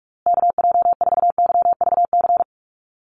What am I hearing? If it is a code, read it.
Morse code, heard as a keyed tone, about 44 wpm, 690 Hz. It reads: KJ4YVC
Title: USA Call Signs Repeated 2x 45wpm